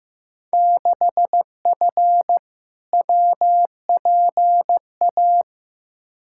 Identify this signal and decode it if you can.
Morse code, heard as a keyed tone, about 15 wpm, 700 Hz. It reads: 6F WPA
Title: Field Day Exchanges Encore - 15wpm